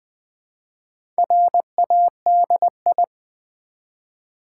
Morse code, heard as a keyed tone, about 20 wpm, 700 Hz. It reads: RADI
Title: Build A Word - 1 Letter At A Time - 5-Letter Words - 20wpm